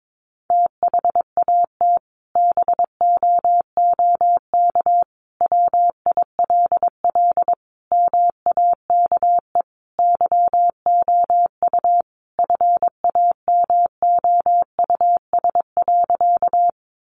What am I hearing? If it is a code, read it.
Morse code, heard as a keyed tone, about 22 wpm, 700 Hz. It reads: THAT BOOK WILL MAKE YOU FAMOUS.
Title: Sentences from Top 1500 Words 22wpm